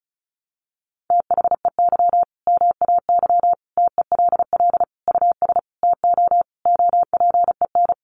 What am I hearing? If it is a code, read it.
Morse code, heard as a keyed tone, about 34 wpm, 700 Hz. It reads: THEY MAY TELL US TO OPEN
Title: Sentences from Top 300 Words 35wpm